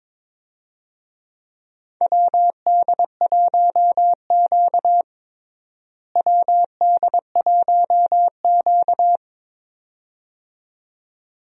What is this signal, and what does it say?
Morse code, heard as a keyed tone, about 21 wpm, 700 Hz. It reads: WD1Q WD1Q
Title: USA Call Signs - 4 Characters - Repeated 2x - 22wpm